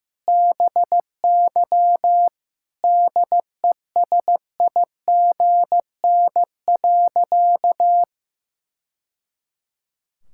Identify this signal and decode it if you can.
Morse code, heard as a keyed tone, about 15 wpm, 700 Hz. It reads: BY DESIGN.
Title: Sentences from Top 1000 Words - Review Part 5 - 15wpm